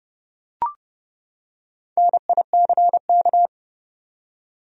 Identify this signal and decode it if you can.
Morse code, heard as a keyed tone, about 30 wpm, 700 Hz. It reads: NICK
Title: USA Common Names - Max 5 Letters - 30wpm